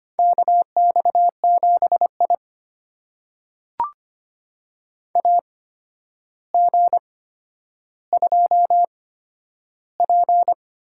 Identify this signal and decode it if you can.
Morse code, heard as a keyed tone, about 25 wpm, spaced as for 11 wpm, 700 Hz. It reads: KX7I AG2P
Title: USA Call Signs - 4 Characters - 8x Character Spacing - 25wpm